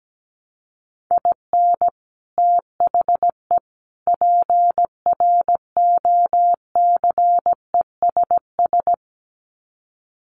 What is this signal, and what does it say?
Morse code, heard as a keyed tone, about 17 wpm, 700 Hz. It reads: IN THE PROCESS
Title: Sets of 3 Words 17wpm